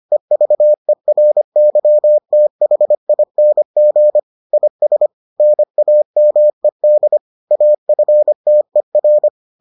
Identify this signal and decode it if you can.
Morse code, heard as a keyed tone, about 25 wpm, 600 Hz. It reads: EVERYTHING IS NAMED AFTER